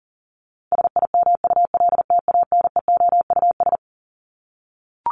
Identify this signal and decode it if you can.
Morse code, heard as a keyed tone, about 40 wpm, 700 Hz. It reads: SIMULTANEOUS
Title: LY Suffix - Top 100 Words with Base Words 40wpm